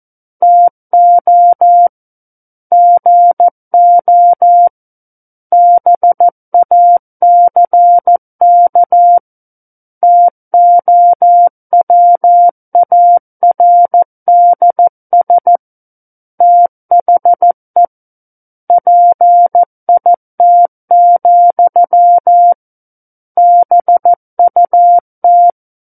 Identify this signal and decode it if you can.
Morse code, heard as a keyed tone, about 14 wpm, 700 Hz. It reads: TO GO BACK TOWARDS THE PIT, BUT
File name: Book One / 05-Chapter 5 - 14 WPM